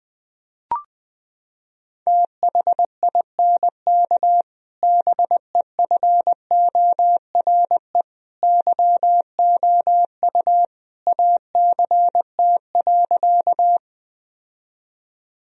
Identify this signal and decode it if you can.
Morse code, heard as a keyed tone, about 20 wpm, 700 Hz. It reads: THINK BEFORE YOU ACT.
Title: Sentences from Top 1000 Words - Review Part 1 - 20wpm